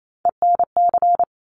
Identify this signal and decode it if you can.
Morse code, heard as a keyed tone, about 28 wpm, 700 Hz. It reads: ENC
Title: Top Three Letter Combinations in Words 28wpm